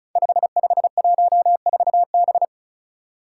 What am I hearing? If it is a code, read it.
Morse code, heard as a keyed tone, about 36 wpm, 700 Hz. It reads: 5514B
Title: Five Challenging Letter-Number Combinations 35wpm